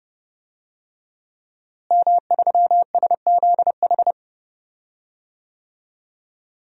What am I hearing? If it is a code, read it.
Morse code, heard as a keyed tone, about 31 wpm, 700 Hz. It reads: M3SZH
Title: DXCC-100 Call Signs - 5 Characters - Repeated 3x - 30wpm